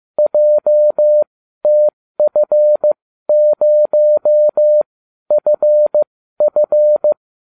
Morse code, heard as a keyed tone, about 15 wpm, spaced as for 12 wpm, 600 Hz. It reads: JTF0FF